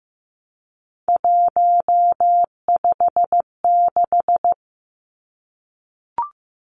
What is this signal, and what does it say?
Morse code, heard as a keyed tone, about 15 wpm, 700 Hz. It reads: 156
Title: Numbers - 3-Digit 15wpm